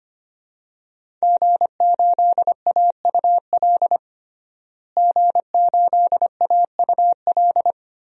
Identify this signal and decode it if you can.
Morse code, heard as a keyed tone, about 25 wpm, 700 Hz. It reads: G8AUL G8AUL
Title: DXCC-100 Call Signs - 5 Characters - Repeated 2x - 25wpm